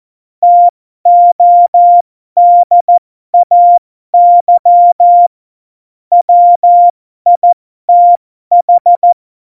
Morse code, heard as a keyed tone, about 14 wpm, spaced as for 12 wpm, 700 Hz. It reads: TODAY WITH